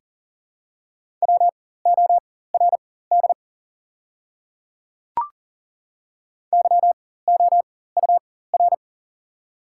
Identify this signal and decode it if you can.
Morse code, heard as a keyed tone, about 38 wpm, spaced as for 18 wpm, 700 Hz. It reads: WORD YOUR
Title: 4x Character Spacing - Top 100 Words - 40wpm